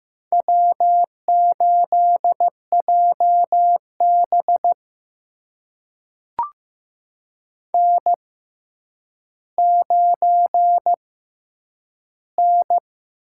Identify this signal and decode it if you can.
Morse code, heard as a keyed tone, about 15 wpm, spaced as for 9 wpm, 700 Hz. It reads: W8JB N9N
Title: USA Call Signs - 4 Characters - 6x Character Spacing - 15wpm